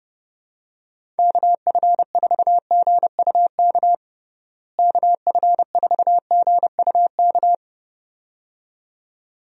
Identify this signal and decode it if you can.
Morse code, heard as a keyed tone, about 30 wpm, 700 Hz. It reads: KF4GUK KF4GUK
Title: USA Call Signs - 6 Characters - Repeated 2x - 30wpm